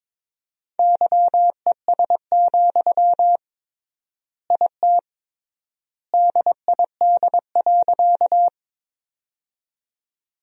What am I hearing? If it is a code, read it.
Morse code, heard as a keyed tone, about 22 wpm, 700 Hz. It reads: YES, IT DID.
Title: 3x Word-Spacing - Sentences from Top 500 Words - 22wpm